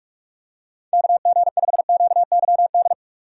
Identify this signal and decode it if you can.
Morse code, heard as a keyed tone, about 46 wpm, 700 Hz. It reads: KG5QYD
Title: USA Call Signs - 4x Character Spacing - 45wpm